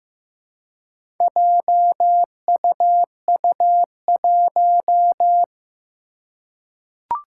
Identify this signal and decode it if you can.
Morse code, heard as a keyed tone, about 15 wpm, 700 Hz. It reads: JUU1
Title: Four Challenging Letter-Number Combinations 15wpm